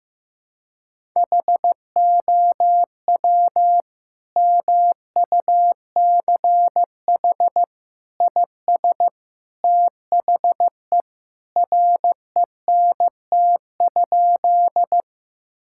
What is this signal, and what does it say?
Morse code, heard as a keyed tone, about 15 wpm, 700 Hz. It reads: HOW MUCH IS THE RENT?
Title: Sentences from Top 1000 Words - Review Part 5 - 15wpm